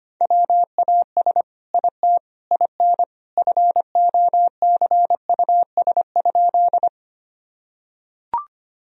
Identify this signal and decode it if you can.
Morse code, heard as a keyed tone, about 25 wpm, 700 Hz. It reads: WAS IT IN FOCUS?
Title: Sentences from Top 1000 Words - Review Part 4 - 25wpm